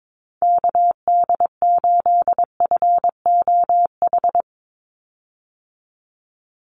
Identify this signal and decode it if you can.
Morse code, heard as a keyed tone, about 22 wpm, 700 Hz. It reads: KD8FOH
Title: USA Call Signs 22wpm